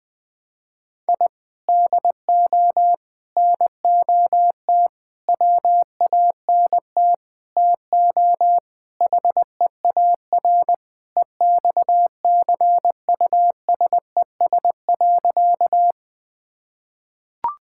Morse code, heard as a keyed tone, about 20 wpm, 700 Hz. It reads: I DO NOT WANT TO HEAR EXCUSES.
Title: Sentences with 5-Word Phrases - Spoken Beginning - 20wpm